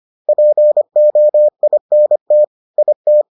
Morse code, heard as a keyed tone, about 25 wpm, 600 Hz. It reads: POINT IT